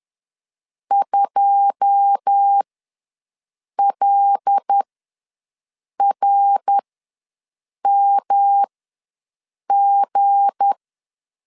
Morse code, heard as a keyed tone, about 10 wpm, spaced as for 6 wpm, 800 Hz. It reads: 2LRMG
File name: LEZIONI / 9 2 ws 10wpm 4130